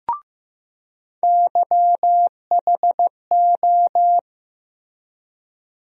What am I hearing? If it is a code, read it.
Morse code, heard as a keyed tone, about 15 wpm, 700 Hz. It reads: YHO
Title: USA Call Sign Postfixes - 3 Letters 15wpm